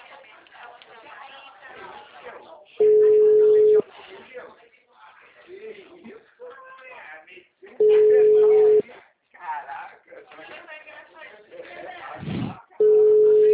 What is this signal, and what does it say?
Ringback tone (Europe, 425 Hz), with speech.